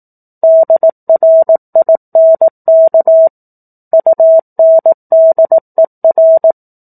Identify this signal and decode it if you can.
Morse code, heard as a keyed tone, about 18 wpm, 650 Hz. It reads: DRINK UNDER
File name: 04-Chapter-04 18 WPM